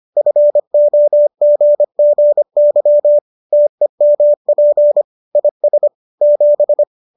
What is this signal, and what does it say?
Morse code, heard as a keyed tone, about 25 wpm, 590 Hz. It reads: FOGGY TEMP IS 7